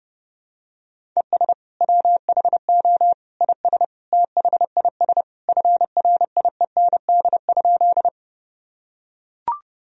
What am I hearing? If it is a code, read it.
Morse code, heard as a keyed tone, about 30 wpm, 700 Hz. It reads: ES WHO IS THIS FRIEND?
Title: Sentences from Top 300 Words and Abbreviations 30wpm